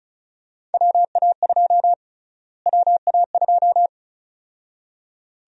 Morse code, heard as a keyed tone, about 36 wpm, 700 Hz. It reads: WA2 WA2
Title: USA Call Sign Prefixes - Repeated 2x - 35wpm